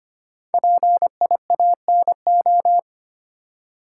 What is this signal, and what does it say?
Morse code, heard as a keyed tone, about 25 wpm, 700 Hz. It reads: PIANO
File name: 5 Letter Words - First 3 Letters Given - 25wpm